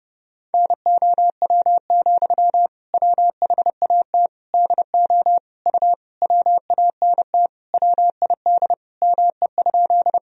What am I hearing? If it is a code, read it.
Morse code, heard as a keyed tone, about 32 wpm, 700 Hz. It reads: NOW, WHAT DO U WANT WID ME?
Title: Sentences from Top 200 Words and Abbreviations 30wpm